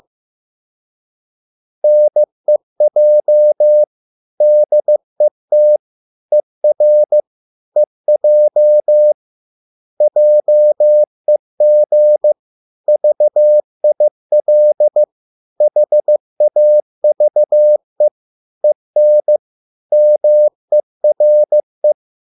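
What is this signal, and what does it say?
Morse code, heard as a keyed tone, about 15 wpm, 600 Hz. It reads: NEJ DET ER EJ JEG VIL HAVE EN MERE
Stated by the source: Morse code at a slow 15 wpm from CW med Gnister.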